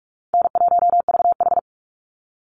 Morse code, heard as a keyed tone, about 44 wpm, 700 Hz. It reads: N1VH
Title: Easy Call Signs - 45wpm